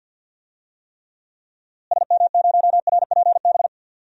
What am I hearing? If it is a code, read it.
Morse code, heard as a keyed tone, about 52 wpm, 700 Hz. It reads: SM0RPB
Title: DXCC-100 Call Signs - 6 Characters - Repeated 3x - 50wpm